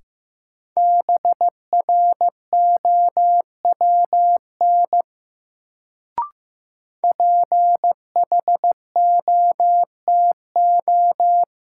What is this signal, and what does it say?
Morse code, heard as a keyed tone, about 15 wpm, 700 Hz. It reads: BROWN PHOTO
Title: Sentences from Top 1200 Words - New Word Drill - 15wpm